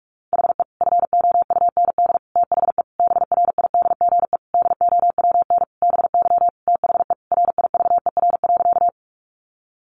Morse code, heard as a keyed tone, about 44 wpm, 700 Hz. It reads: HE FOUND THE BRIDGE DOWN BY THE RIVER.
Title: Sentences from Top 1200 Words 45wpm